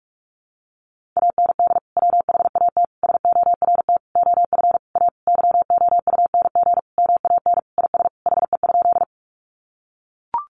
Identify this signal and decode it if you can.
Morse code, heard as a keyed tone, about 44 wpm, 700 Hz. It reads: AND WHAT SORT OF A YOUNG MAN IS HE?
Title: Sentences from Top 1300 Words 45wpm